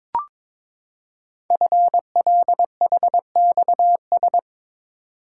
Morse code, heard as a keyed tone, about 23 wpm, 700 Hz. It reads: FLHXS
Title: Code Groups - 5 Letters - 22wpm